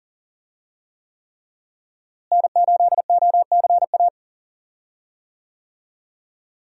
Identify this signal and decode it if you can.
Morse code, heard as a keyed tone, about 40 wpm, 700 Hz. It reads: N8OCA
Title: USA Call Signs - 5 Characters - Repeated 3x - 40wpm